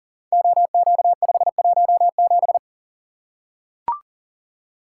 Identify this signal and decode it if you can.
Morse code, heard as a keyed tone, about 40 wpm, 700 Hz. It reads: OQ517
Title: Code Groups - 5 Letter:Numbers - 40wpm